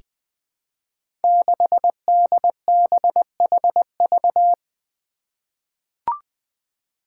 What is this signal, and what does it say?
Morse code, heard as a keyed tone, about 21 wpm, 700 Hz. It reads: 6DBHV
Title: Five Challenging Letter-Number Combinations 20wpm